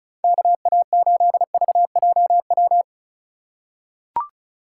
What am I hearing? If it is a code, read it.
Morse code, heard as a keyed tone, about 36 wpm, 700 Hz. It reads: KA8VJW
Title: USA Call Signs - 6 Characters - 4x Character Spacing - 35wpm